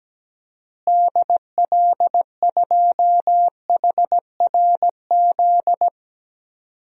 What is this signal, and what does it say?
Morse code, heard as a keyed tone, about 17 wpm, 700 Hz. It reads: DL2HRZ
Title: DXCC-100 Call Signs - 6 Characters - Repeated 3x - 17wpm